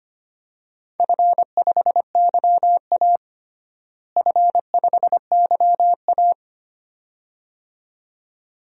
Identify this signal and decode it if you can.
Morse code, heard as a keyed tone, about 25 wpm, 700 Hz. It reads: F5YA F5YA
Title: DXCC-100 Call Signs - 4 Characters - Repeated 2x - 25wpm